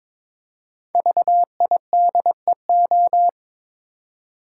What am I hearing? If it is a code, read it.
Morse code, heard as a keyed tone, about 22 wpm, 700 Hz. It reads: VIDEO
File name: 5 Letter Words - First 3 Letters Given - 22wpm